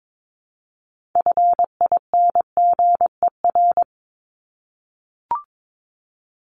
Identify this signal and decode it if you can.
Morse code, heard as a keyed tone, about 22 wpm, 700 Hz. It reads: FINGER
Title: Top 1501-1750 Words 22wpm